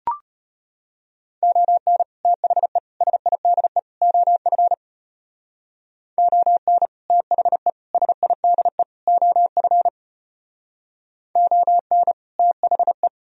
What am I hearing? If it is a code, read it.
Morse code, heard as a keyed tone, about 32 wpm, 700 Hz. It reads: ON THE SIDE OF ON THE SIDE OF ON THE